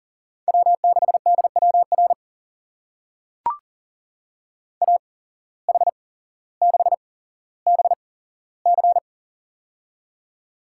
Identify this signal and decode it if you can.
Morse code, heard as a keyed tone, about 40 wpm, spaced as for 17 wpm, 700 Hz. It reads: W6DWR AH6BC
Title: USA Call Signs - 8x Character Spacing - 40wpm